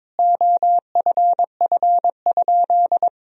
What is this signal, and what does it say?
Morse code, heard as a keyed tone, about 21 wpm, 700 Hz. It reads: OFF?
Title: Sentences from Top 1000 Words - Review Part 3 - 22wpm